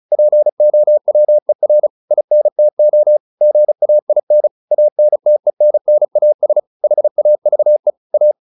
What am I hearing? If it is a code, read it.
Morse code, heard as a keyed tone, about 34 wpm, 600 Hz. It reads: POWER INTO GAIN ANTENNAS HAVE A